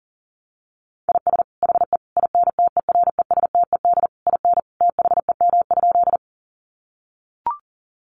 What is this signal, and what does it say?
Morse code, heard as a keyed tone, about 40 wpm, 700 Hz. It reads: IS HE INTERESTED IN THEM?
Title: Sentences from Top 900 Words 40wpm